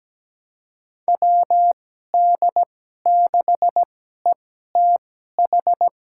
Morse code, heard as a keyed tone, about 17 wpm, spaced as for 12 wpm, 700 Hz. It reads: WD6ETH
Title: USA Call Signs - 6 Characters - 2x Character Spacing - 17wpm